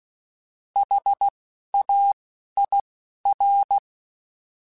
Morse code, heard as a keyed tone, about 16 wpm, 800 Hz. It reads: H A I R